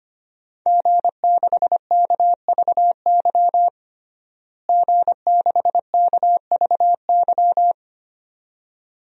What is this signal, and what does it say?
Morse code, heard as a keyed tone, about 25 wpm, 700 Hz. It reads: G6KVY G6KVY
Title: DXCC-100 Call Signs - 5 Characters - Repeated 2x - 25wpm